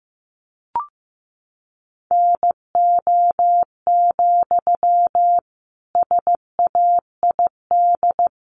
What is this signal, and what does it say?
Morse code, heard as a keyed tone, about 15 wpm, 700 Hz. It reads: NO, SAID
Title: Sentences from Top 1500 Words 15wpm